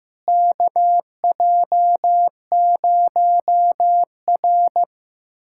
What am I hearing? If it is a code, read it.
Morse code, heard as a keyed tone, about 15 wpm, 700 Hz. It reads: KJ0R
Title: USA Callsign DE Callsign - 4 Characters - 15wpm